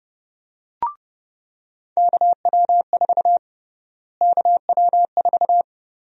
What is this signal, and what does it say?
Morse code, heard as a keyed tone, about 30 wpm, 700 Hz. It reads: KW4 KW4
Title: USA Call Sign Prefixes - Repeated 2x - 30wpm